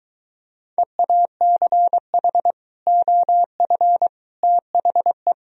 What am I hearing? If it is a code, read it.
Morse code, heard as a keyed tone, about 23 wpm, 700 Hz. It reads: EACH OF THE